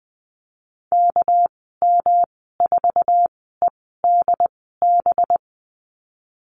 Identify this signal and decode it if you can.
Morse code, heard as a keyed tone, about 20 wpm, spaced as for 14 wpm, 700 Hz. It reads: KM4EDB
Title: 245 Lesson 39g - Letter z - Call Signs 20wpm